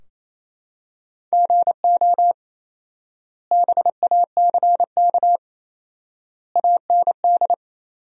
Morse code, heard as a keyed tone, about 28 wpm, 700 Hz. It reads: GO BACK AND
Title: 4x Word-Spacing - Sentences from Top 300 Words - 28wpm